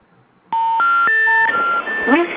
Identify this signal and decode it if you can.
Special information tone, with speech.
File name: recordings / RE4d444b49106a63ca c a3